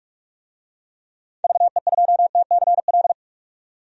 Morse code, heard as a keyed tone, about 44 wpm, 700 Hz. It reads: VE2TCL
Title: DXCC-100 Call Signs - 6 Characters - 45wpm